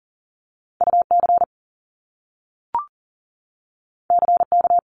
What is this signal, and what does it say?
Morse code, heard as a keyed tone, about 41 wpm, 700 Hz. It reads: UC CK
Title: Top Two Letter Combinations in Words 40wpm